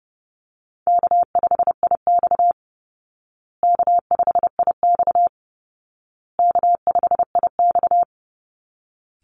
Morse code, heard as a keyed tone, about 30 wpm, 700 Hz. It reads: K5IX K5IX K5IX
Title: Call Signs Repeated 2x - 30wpm